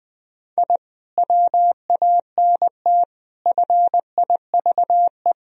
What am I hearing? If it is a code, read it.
Morse code, heard as a keyed tone, about 20 wpm, 700 Hz. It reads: I WANT FIVE
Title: Sentences from Top 1000 Words - Review Part 2 - 20wpm